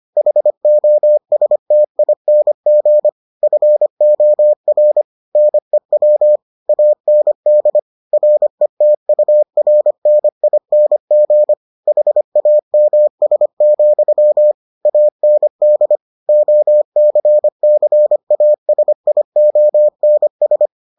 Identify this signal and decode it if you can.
Morse code, heard as a keyed tone, about 25 wpm, 600 Hz. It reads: HOSTING FOR NEW AND RETURNING HAMS, AND OCCASIONS